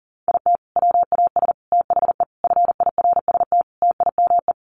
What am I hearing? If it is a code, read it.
Morse code, heard as a keyed tone, about 40 wpm, 700 Hz. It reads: IT WAS THE FIRST TIME
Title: Sets of 5 Words 40wpm